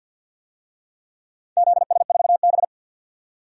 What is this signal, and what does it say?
Morse code, heard as a keyed tone, about 48 wpm, 700 Hz. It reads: ZS4B
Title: DXCC-100 Call Signs - 4 Characters - 50wpm